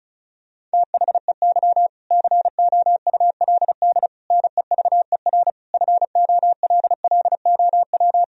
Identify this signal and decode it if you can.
Morse code, heard as a keyed tone, about 36 wpm, 700 Hz. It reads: THEY COULD NEVER FOLLOW
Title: Sentences from Top 1000 Words - Review Part 4 - 35wpm